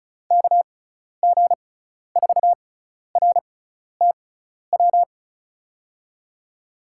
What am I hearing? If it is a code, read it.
Morse code, heard as a keyed tone, about 36 wpm, spaced as for 12 wpm, 700 Hz. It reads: KG4RTW
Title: USA Call Signs - 6 Characters - 6x Character Spacing - 35wpm